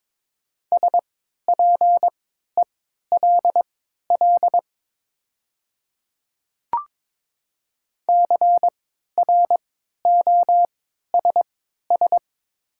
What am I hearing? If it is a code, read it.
Morse code, heard as a keyed tone, about 22 wpm, spaced as for 12 wpm, 700 Hz. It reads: SPELL CROSS
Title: 3x Character Spacing - Top 300 Words - 22wpm